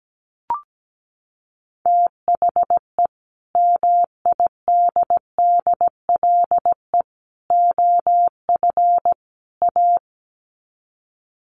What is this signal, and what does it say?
Morse code, heard as a keyed tone, about 17 wpm, 700 Hz. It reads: THE MIDDLE OF A
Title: Sets of 4 Words 17wpm